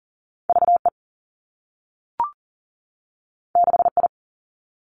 Morse code, heard as a keyed tone, about 42 wpm, 700 Hz. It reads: V E 6 I
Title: Two Challenging Letter-Number Combinations 40wpm